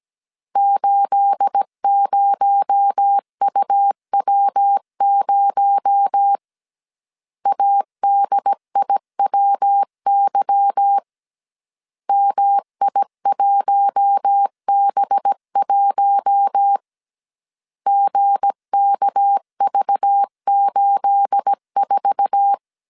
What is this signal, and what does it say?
Morse code, heard as a keyed tone, about 17 wpm, 800 Hz. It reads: 80UW0 ADIWY MI1B1 GKV84
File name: CW / DET / 1 9 1500C S A 16wpm 2828